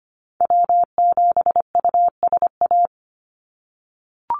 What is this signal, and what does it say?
Morse code, heard as a keyed tone, about 25 wpm, 700 Hz. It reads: W7USA
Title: NA SOTA Chaser Callsigns 25wpm